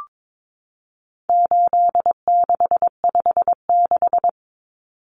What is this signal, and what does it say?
Morse code, heard as a keyed tone, about 22 wpm, 700 Hz. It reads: 8656
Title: Common Cut Numbers - 4-Digit 22wpm